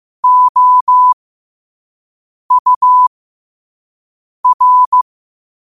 Morse code, heard as a keyed tone, about 15 wpm, spaced as for 5 wpm, 1000 Hz. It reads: OUR
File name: qotdfiles / 5wpm